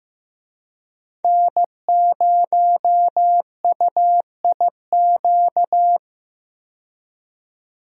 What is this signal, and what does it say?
Morse code, heard as a keyed tone, about 15 wpm, 700 Hz. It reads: N0UIQ
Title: USA Call Signs - 5 Characters - 15wpm